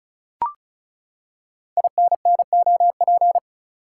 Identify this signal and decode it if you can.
Morse code, heard as a keyed tone, about 36 wpm, 690 Hz. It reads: INNOP
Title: Code Groups - 5 Letters - 35wpm